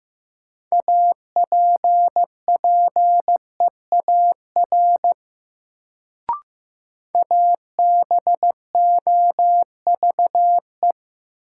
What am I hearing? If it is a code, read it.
Morse code, heard as a keyed tone, about 15 wpm, 700 Hz. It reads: APPEAR ABOVE
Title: Sentences from Top 1000 Words - Review Part 2 - New Word Drill - 15wpm